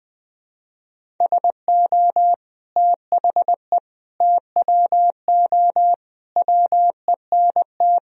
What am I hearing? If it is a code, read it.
Morse code, heard as a keyed tone, about 20 wpm, 700 Hz. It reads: SO THE TWO WENT
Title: Sentences from Top 1000 Words - Review Part 1 - 20wpm